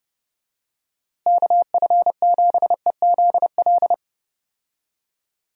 Morse code, heard as a keyed tone, about 30 wpm, 700 Hz. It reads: KF7EZL
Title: USA Call Signs - 6 Characters - Repeated 2x - 30wpm